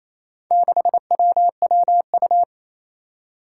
Morse code, heard as a keyed tone, about 28 wpm, 700 Hz. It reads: 6WWU
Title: Four Challenging Letter-Number Combinations 28wpm